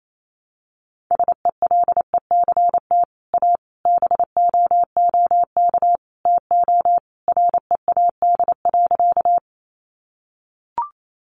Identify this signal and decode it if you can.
Morse code, heard as a keyed tone, about 28 wpm, 700 Hz. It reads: SELECT A BOOK TO READ.
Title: Sentences from Top 1300 Words 28wpm